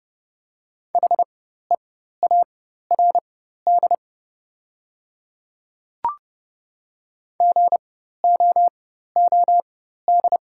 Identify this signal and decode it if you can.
Morse code, heard as a keyed tone, about 30 wpm, spaced as for 14 wpm, 700 Hz. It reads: HEARD GOOD
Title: 4x Character Spacing - Top 500 Words - 30wpm